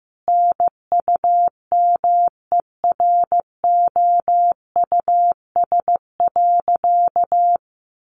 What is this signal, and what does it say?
Morse code, heard as a keyed tone, about 15 wpm, 700 Hz. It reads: NUMEROUS.
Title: Sentences from Top 1400 Words 15wpm